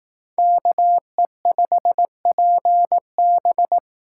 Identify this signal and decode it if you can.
Morse code, heard as a keyed tone, about 18 wpm, 700 Hz. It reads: KE5PB
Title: Speed Racing - USA 2x2 Call Signs 15wpm